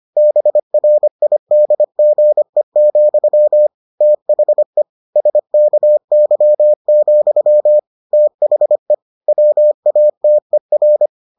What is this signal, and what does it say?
Morse code, heard as a keyed tone, about 25 wpm, 600 Hz. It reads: BRIDGE, THE SKY, THE WATER